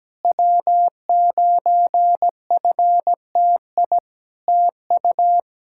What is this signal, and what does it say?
Morse code, heard as a keyed tone, about 17 wpm, 700 Hz. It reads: W9FTI TU
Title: Practice QSO - Exchange 4 - Ending - 17wpm